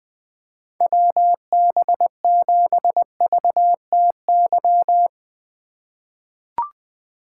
Morse code, heard as a keyed tone, about 20 wpm, 700 Hz. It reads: WB7VTY
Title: 2023 SOTA Chaser Call Signs - 20wpm